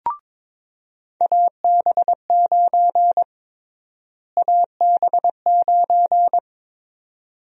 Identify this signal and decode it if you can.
Morse code, heard as a keyed tone, about 21 wpm, 700 Hz. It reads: AB9 AB9
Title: USA Call Sign Prefixes - Repeated 2x - 22wpm